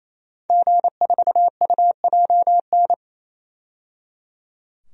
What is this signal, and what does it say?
Morse code, heard as a keyed tone, about 28 wpm, 700 Hz. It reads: G4UJN
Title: DXCC-100 Call Signs - 5 Characters - 28wpm